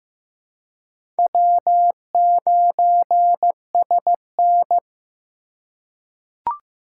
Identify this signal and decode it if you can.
Morse code, heard as a keyed tone, about 15 wpm, 700 Hz. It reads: W9SN
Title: USA Call Signs - 4 Characters - 2x Character Spacing - 15wpm